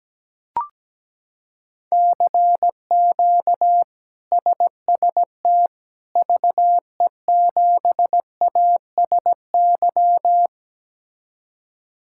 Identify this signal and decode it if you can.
Morse code, heard as a keyed tone, about 17 wpm, 700 Hz. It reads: CQ SST VE7ASY
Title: K1USN SST QSO Practice - 17wpm